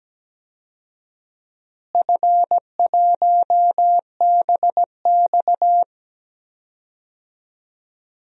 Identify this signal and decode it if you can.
Morse code, heard as a keyed tone, about 17 wpm, 700 Hz. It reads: F1BX
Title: DXCC-100 Call Signs - 4 Characters - Repeated 2x - 17wpm